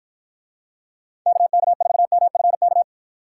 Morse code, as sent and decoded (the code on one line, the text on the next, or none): -.- -.- ....- -- ..-. -.-
KK4MFK